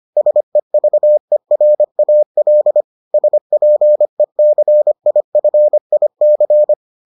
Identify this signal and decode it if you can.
Morse code, heard as a keyed tone, about 25 wpm, 600 Hz. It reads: SEVERAL SPECIFIC